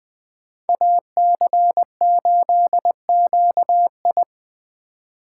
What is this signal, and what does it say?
Morse code, heard as a keyed tone, about 20 wpm, 700 Hz. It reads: AC8QI